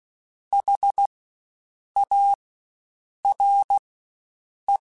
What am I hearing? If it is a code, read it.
Morse code, heard as a keyed tone, about 16 wpm, spaced as for 7 wpm, 800 Hz. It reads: HARE
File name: Spaced Alphas / Spaced Alphas mp3s / Hs sp @15